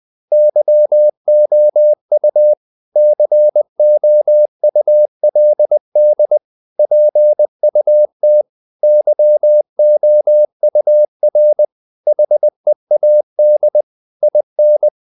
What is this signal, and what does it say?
Morse code, heard as a keyed tone, about 20 wpm, 600 Hz. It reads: YOU COULD PUT YOUR HEAD IN
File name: the old man and the sea part 2 20wpm